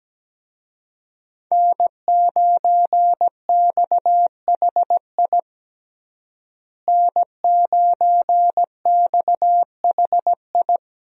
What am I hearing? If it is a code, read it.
Morse code, heard as a keyed tone, about 17 wpm, 700 Hz. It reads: N9XHI N9XHI
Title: USA Call Signs - 5 Characters - Repeated 3x - 17wpm